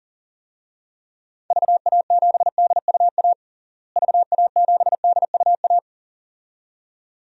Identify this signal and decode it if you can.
Morse code, heard as a keyed tone, about 40 wpm, 700 Hz. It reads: VA7DUA VA7DUA
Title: DXCC-100 Call Signs - 6 Characters - Repeated 2x - 40wpm